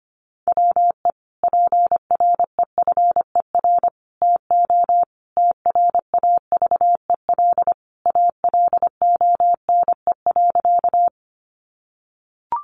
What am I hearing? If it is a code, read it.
Morse code, heard as a keyed tone, about 25 wpm, 700 Hz. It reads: WE PREFER TO TRAVEL ALONE.
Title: Sentences from Top 1100 Words 25wpm